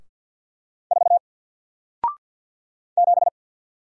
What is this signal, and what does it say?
Morse code, heard as a keyed tone, about 48 wpm, 700 Hz. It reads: 4 7
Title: Single Character (Letter:Number:Punctuation:Prosign) Alt 50wpm